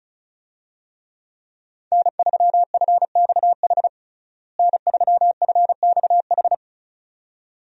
Morse code, heard as a keyed tone, about 34 wpm, 700 Hz. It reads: N3FXH N3FXH
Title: USA Call Signs - 5 Characters - Repeated 2x - 35wpm